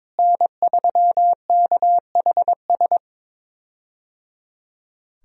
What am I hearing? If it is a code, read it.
Morse code, heard as a keyed tone, about 22 wpm, 700 Hz. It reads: N3KHS
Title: USA Call Signs - 5 Characters - Repeated 3x - 22wpm